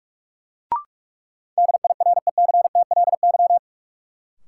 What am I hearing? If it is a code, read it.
Morse code, heard as a keyed tone, about 45 wpm, 700 Hz. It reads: DIRECTLY
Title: Sentences from Top 800 Words - New Word Drill - 45wpm